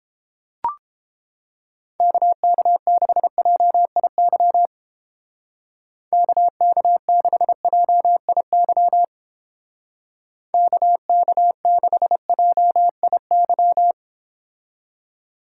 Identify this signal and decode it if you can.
Morse code, heard as a keyed tone, about 30 wpm, 700 Hz. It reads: KK6JIY KK6JIY KK6JIY